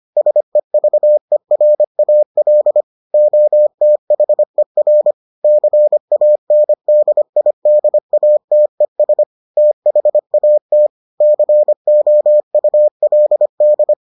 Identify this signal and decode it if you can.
Morse code, heard as a keyed tone, about 25 wpm, 600 Hz. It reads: SEVERAL OTHER CANDIDATES THAT COULD